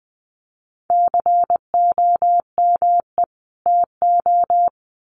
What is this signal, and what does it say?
Morse code, heard as a keyed tone, about 20 wpm, 700 Hz. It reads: COME TO
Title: Sets of 2 Words 20wpm